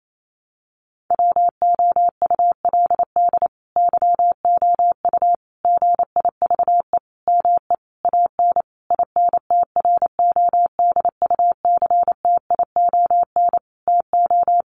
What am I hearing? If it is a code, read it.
Morse code, heard as a keyed tone, about 28 wpm, 700 Hz. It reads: WOULD YOU GIVE ME AN INTRODUCTION TO
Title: Sentences from Top 1000 Words 28wpm